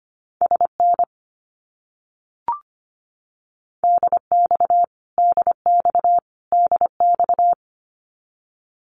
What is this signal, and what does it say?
Morse code, heard as a keyed tone, about 25 wpm, 700 Hz. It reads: SN DX DX DX
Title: Abbreviations Repeated 3x - 25wpm